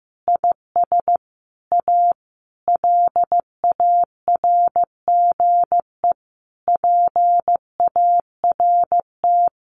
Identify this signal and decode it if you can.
Morse code, heard as a keyed tone, about 15 wpm, 700 Hz. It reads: IS A LARGE PART
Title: Sentences from Top 1400 Words 15wpm